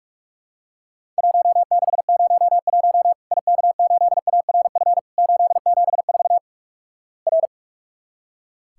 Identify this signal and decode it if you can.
Morse code, heard as a keyed tone, about 46 wpm, 700 Hz. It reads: 1601 IK8ARF 874 R
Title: Worked All Europe DX Contest - QTC Exchange - 45wpm